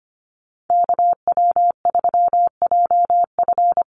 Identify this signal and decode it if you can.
Morse code, heard as a keyed tone, about 25 wpm, 700 Hz. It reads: KW3JF
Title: USA Call Signs Repeated 3x 25wpm